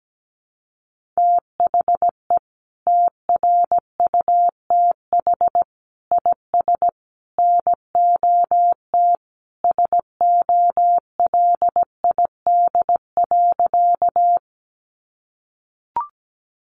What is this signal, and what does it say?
Morse code, heard as a keyed tone, about 17 wpm, 700 Hz. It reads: THE TRUTH IS NOT SOLID.
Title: Sentences from Top 1500 Words 17wpm